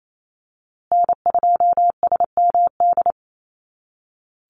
Morse code, heard as a keyed tone, about 28 wpm, 700 Hz. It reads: N2SMD
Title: USA Call Signs Repeated 2x 28wpm